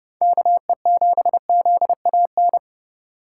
Morse code, heard as a keyed tone, about 30 wpm, 700 Hz. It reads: KE7ZAN
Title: USA Call Signs - 6 Characters - 4x Character Spacing - 30wpm